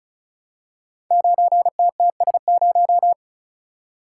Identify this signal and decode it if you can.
Morse code, heard as a keyed tone, about 34 wpm, 700 Hz. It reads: 9TTS0
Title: Code Groups - 5 Letter:Numbers - 35wpm